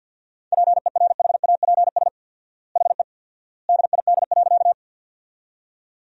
Morse code, heard as a keyed tone, about 53 wpm, 700 Hz. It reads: PERHAPS HE DID.
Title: 4x Word-Spacing - Sentences from Top 500 Words - 50wpm